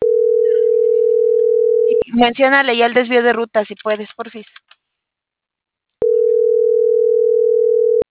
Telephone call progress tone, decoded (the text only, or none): ringback tone